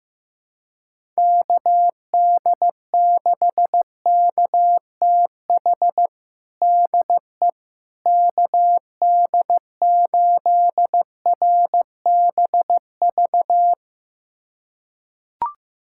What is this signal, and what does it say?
Morse code, heard as a keyed tone, about 15 wpm, 700 Hz. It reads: KD6KTH DE KD8RBV
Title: USA Callsign DE Callsign - 6 Characters - 15wpm